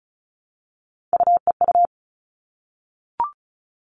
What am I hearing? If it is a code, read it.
Morse code, heard as a keyed tone, about 34 wpm, 700 Hz. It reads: UEU
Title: All Cut Numbers - 3-Digit 35wpm